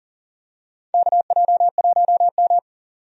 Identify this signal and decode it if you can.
Morse code, heard as a keyed tone, about 40 wpm, 700 Hz. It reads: KJ1M
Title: USA Call Signs - 4 Characters - 40wpm